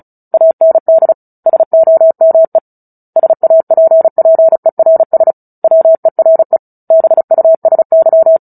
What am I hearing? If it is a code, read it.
Morse code, heard as a keyed tone, about 34 wpm, 650 Hz. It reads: AND SOME SAPPERS WERE BUSY
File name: Book Two / 09-Chapter 9 - 35 WPM